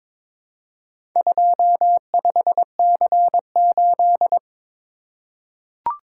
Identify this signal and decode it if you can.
Morse code, heard as a keyed tone, about 23 wpm, 700 Hz. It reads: 25C8
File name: practice / 22wpm / Four Letter-Number Combinations - 22wpm